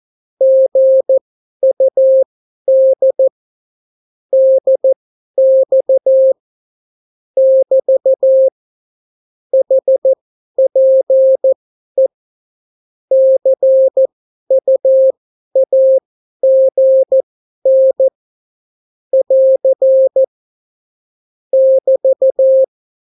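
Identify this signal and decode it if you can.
Morse code, heard as a keyed tone, about 14 wpm, spaced as for 11 wpm, 530 Hz. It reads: GUD DX <BT> HPE CUAGN <AR> <BT>
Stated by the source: Example Spanish CW QSO from a Morse learning book, 14 wpm characters with 8 wpm spacing.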